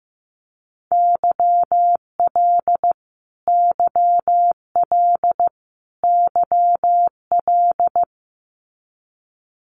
Morse code, heard as a keyed tone, about 14 wpm, 700 Hz. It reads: YL YL YL
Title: Abbreviations Repeated 3x - 15wpm